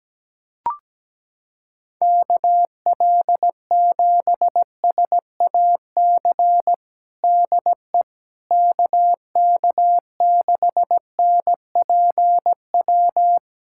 Morse code, heard as a keyed tone, about 17 wpm, 700 Hz. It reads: KL7SAC DE KK6NPW
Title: USA Callsign DE Callsign - 6 Characters - 17wpm